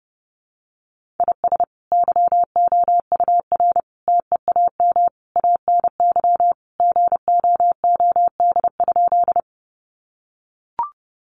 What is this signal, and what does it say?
Morse code, heard as a keyed tone, about 30 wpm, 700 Hz. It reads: IS YOUR TEAM ANY GOOD?
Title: Sentences from Top 1500 Words 30wpm